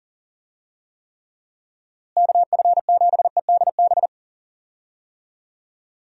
Morse code, heard as a keyed tone, about 39 wpm, 700 Hz. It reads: KF7EDB
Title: USA Call Signs - 6 Characters - Repeated 2x - 40wpm